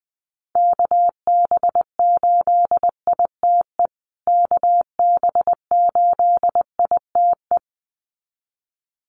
Morse code, heard as a keyed tone, about 19 wpm, 700 Hz. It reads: KB8ITE KB8ITE
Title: USA Call Signs Repeated 2x 20wpm